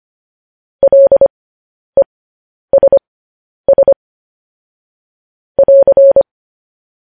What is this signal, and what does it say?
Morse code, heard as a keyed tone, about 25 wpm, spaced as for 10 wpm, 550 Hz. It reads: LESS <AR>
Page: News Headlines in Morse Code at 10 WPM